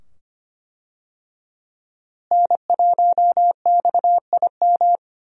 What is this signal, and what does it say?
Morse code, heard as a keyed tone, about 25 wpm, 700 Hz. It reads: N1XIM
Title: USA Call Signs - 5 Characters - Repeated 2x - 25wpm